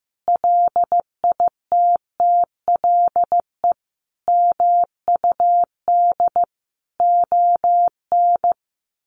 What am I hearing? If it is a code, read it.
Morse code, heard as a keyed tone, about 15 wpm, 700 Hz. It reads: LITTLE MUD ON
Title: A Sound Of Thunder 15wpm